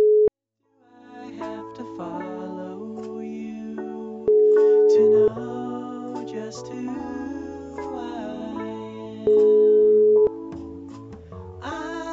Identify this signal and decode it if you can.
Ringback tone (Europe, 425 Hz).